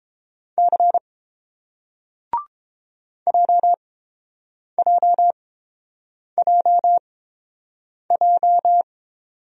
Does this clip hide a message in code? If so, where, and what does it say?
Yes: Morse code from 0.6 to 8.8 s, heard as a keyed tone, about 30 wpm, spaced as for 7 wpm, 700 Hz. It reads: C JJJJ